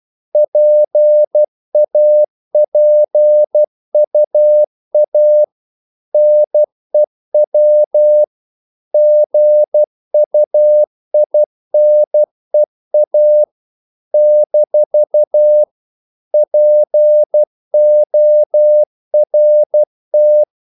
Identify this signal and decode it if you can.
Morse code, heard as a keyed tone, about 12 wpm, 600 Hz. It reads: PAPUA NEW GUINEA - PORT
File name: CW / Practice / NationalCapitals12